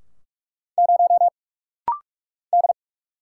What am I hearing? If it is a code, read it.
Morse code, heard as a keyed tone, about 44 wpm, 690 Hz. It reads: 0 D
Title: Single Letter-Number - Rapid-Fire 45wpm